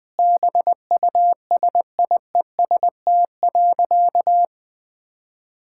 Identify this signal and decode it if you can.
Morse code, heard as a keyed tone, about 20 wpm, 700 Hz. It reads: BUSIEST.
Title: Sentences with 7-Word Phrases - 20wpm